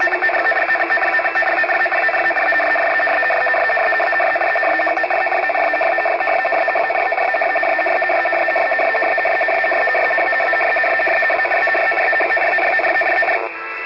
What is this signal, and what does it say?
Two-tone teleprinter signal, about 450 Hz shift.
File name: UKR-RUS-radio-war- -1151utc-7055lsb-UKR-loop-RUS-jamming